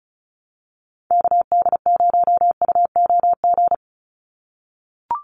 Morse code, heard as a keyed tone, about 34 wpm, 700 Hz. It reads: KD0UOG
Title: USA Call Signs 35wpm